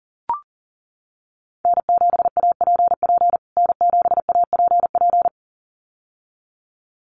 Morse code, heard as a keyed tone, about 40 wpm, 700 Hz. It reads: N7APP N7APP
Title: USA Call Signs Repeated 2x 40wpm